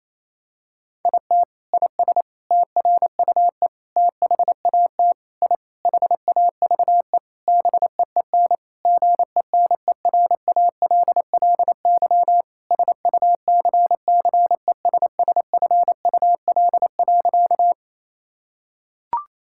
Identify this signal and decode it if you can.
Morse code, heard as a keyed tone, about 27 wpm, 700 Hz. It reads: IT IS TRUE THAT I HAVE BEEN GENERALLY SUCCESSFUL.
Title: Sentences from Top 1000 Words - Review Part 6 - 28wpm